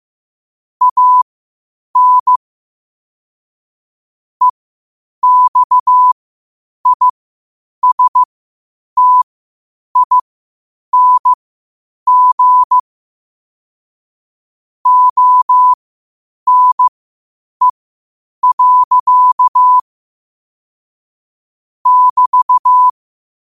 Morse code, heard as a keyed tone, about 15 wpm, spaced as for 8 wpm, 1000 Hz. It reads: AN EXISTING ONE. <BT>
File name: qotdfiles / 8wpm